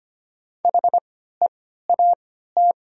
Morse code, heard as a keyed tone, about 25 wpm, spaced as for 14 wpm, 700 Hz. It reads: HEAT
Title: 3x Character Spacing - Top 500 Words - 25wpm